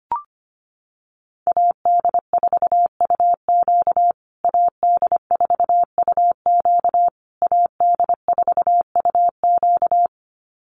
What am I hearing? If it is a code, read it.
Morse code, heard as a keyed tone, about 25 wpm, 700 Hz. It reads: AD4UQ AD4UQ AD4UQ
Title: USA Call Signs Repeated 3x 25wpm